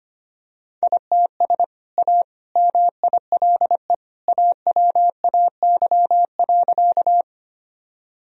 Morse code, heard as a keyed tone, about 25 wpm, 700 Hz. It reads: ITS A MILE AWAY.
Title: Sentences from Top 1000 Words - Review Part 1 - 25wpm